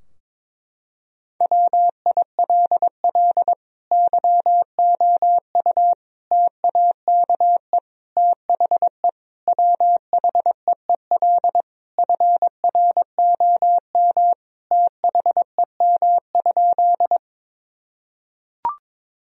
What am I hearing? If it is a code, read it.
Morse code, heard as a keyed tone, about 22 wpm, 700 Hz. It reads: WILL YOU TAKE THE WHEEL FROM THEM?
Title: Sentences from Top 1000 Words - Review Part 3 - 22wpm